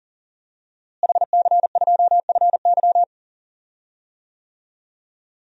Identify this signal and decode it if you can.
Morse code, heard as a keyed tone, about 40 wpm, 700 Hz. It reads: HC2FY
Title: DXCC-100 Call Signs - 5 Characters - Repeated 2x - 40wpm